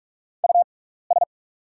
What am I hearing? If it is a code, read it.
Morse code, heard as a keyed tone, about 44 wpm, spaced as for 15 wpm, 690 Hz. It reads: US
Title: 6x Character Spacing - Top 200 Words - 45wpm